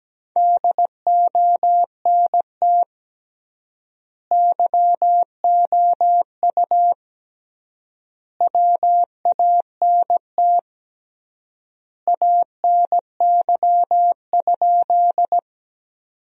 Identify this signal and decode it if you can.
Morse code, heard as a keyed tone, about 17 wpm, 700 Hz. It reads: DONT YOU WANT ANY?
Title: 3x Word-Spacing - Sentences from Top 300 Words - 17wpm